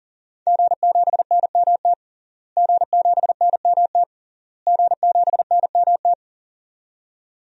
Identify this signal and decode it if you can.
Morse code, heard as a keyed tone, about 40 wpm, 700 Hz. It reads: G7NMT G7NMT G7NMT
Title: DXCC-100 Call Signs - 5 Characters - Repeated 3x - 40wpm